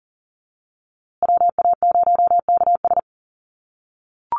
Morse code, heard as a keyed tone, about 40 wpm, 700 Hz. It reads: WA0KS
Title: USA Call Signs Repeated 3x 40wpm